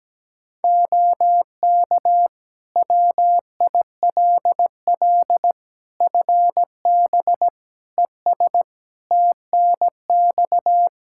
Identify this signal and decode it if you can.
Morse code, heard as a keyed tone, about 17 wpm, 700 Hz. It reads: OK WILL FB ES TNX
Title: Practice QSO - Exchange 2 - Info - 17wpm